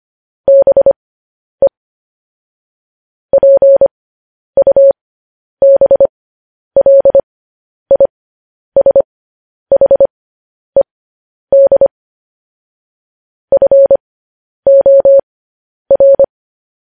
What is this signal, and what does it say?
Morse code, heard as a keyed tone, about 25 wpm, spaced as for 10 wpm, 550 Hz. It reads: BE PUBLISHED FOR
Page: News Headlines in Morse Code at 10 WPM